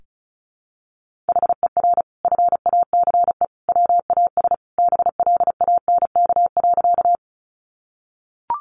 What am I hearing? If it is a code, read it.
Morse code, heard as a keyed tone, about 34 wpm, 700 Hz. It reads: HER FACE WAS BLANK.
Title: Sentences from Top 1500 Words 35wpm